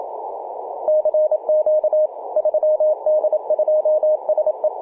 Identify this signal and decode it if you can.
Morse code, heard as a keyed tone, about 27 wpm, 620 Hz, with radio noise. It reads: CQ 3D2SE
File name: 3d2se-18-080mhz-cw